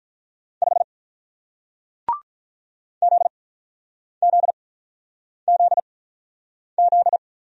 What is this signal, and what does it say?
Morse code, heard as a keyed tone, about 48 wpm, spaced as for 9 wpm, 700 Hz. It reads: 5 ZZZZ